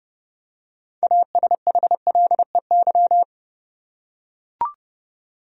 Morse code, heard as a keyed tone, about 30 wpm, 700 Hz. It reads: ASHLEY
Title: USA Common Names - 30wpm